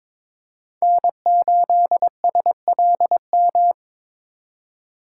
Morse code, heard as a keyed tone, about 22 wpm, 700 Hz. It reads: N8SLM
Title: USA Call Signs - 5 Characters - 8x Character Spacing - 22wpm